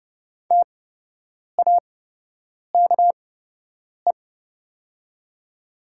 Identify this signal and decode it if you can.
Morse code, heard as a keyed tone, about 30 wpm, spaced as for 8 wpm, 700 Hz. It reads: TAKE
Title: 8x Character Spacing - Top 300 Words - 30wpm